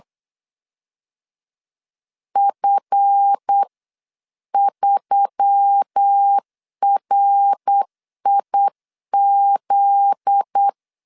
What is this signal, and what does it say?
Morse code, heard as a keyed tone, about 8 wpm, 800 Hz. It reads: F3RIZ
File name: LEZIONI / 7 GF9Z / 7 GF9Z 8wpm 3213